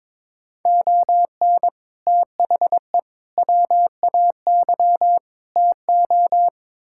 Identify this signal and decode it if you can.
Morse code, heard as a keyed tone, about 22 wpm, 700 Hz. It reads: ON THE WAY TO
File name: practice / 22wpm / Build A Phrase - 5-Word Phrases - 22wpm